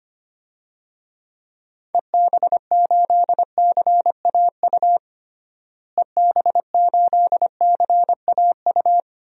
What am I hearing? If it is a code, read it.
Morse code, heard as a keyed tone, about 25 wpm, 700 Hz. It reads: EB8CAU EB8CAU
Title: DXCC-100 Call Signs - 6 Characters - Repeated 3x - 25wpm